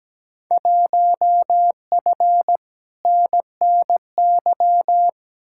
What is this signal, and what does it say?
Morse code, heard as a keyed tone, about 17 wpm, 700 Hz. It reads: 1F NNY
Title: Field Day Exchanges - 17wpm